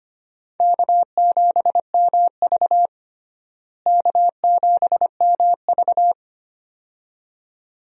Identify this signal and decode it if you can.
Morse code, heard as a keyed tone, about 25 wpm, 700 Hz. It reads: K7MV K7MV
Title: USA Call Signs - 4 Characters - Repeated 2x - 25wpm